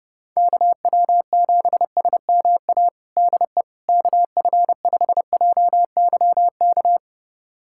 Morse code, heard as a keyed tone, about 32 wpm, 700 Hz. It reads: KW7SMA DE KF5JYK
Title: USA Callsign DE Callsign 30wpm